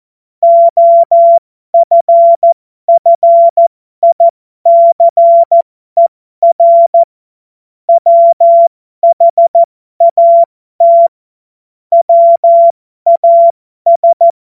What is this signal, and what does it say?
Morse code, heard as a keyed tone, about 14 wpm, spaced as for 12 wpm, 680 Hz. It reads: OFFICER WHAT WAS
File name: Morsekurs / Witze